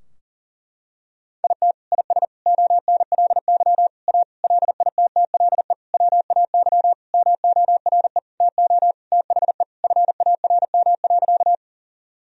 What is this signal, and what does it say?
Morse code, heard as a keyed tone, about 40 wpm, 700 Hz. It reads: IT IS ONLY A LITTLE WAY MORE TO THE FARM.
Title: Sentences from Top 1000 Words - Review Part 2 - 40wpm